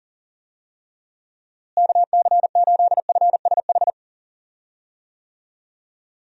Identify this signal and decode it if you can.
Morse code, heard as a keyed tone, about 40 wpm, 700 Hz. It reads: KC8FSH
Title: USA Call Signs - 6 Characters - Repeated 2x - 40wpm